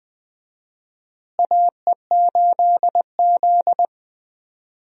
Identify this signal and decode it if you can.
Morse code, heard as a keyed tone, about 20 wpm, 700 Hz. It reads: AE8Z
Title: USA Call Signs - 4 Characters - Repeated 3x - 20wpm